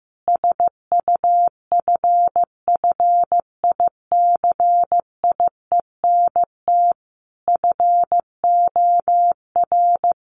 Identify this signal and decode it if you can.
Morse code, heard as a keyed tone, about 15 wpm, 700 Hz. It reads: SUFFICIENT FOR
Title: Sentences from Top 1400 Words 15wpm